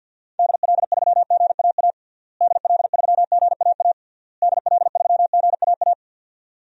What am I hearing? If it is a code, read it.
Morse code, heard as a keyed tone, about 52 wpm, 700 Hz. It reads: DL3GAA DL3GAA DL3GAA
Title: DXCC-100 Call Signs - 6 Characters - Repeated 3x - 50wpm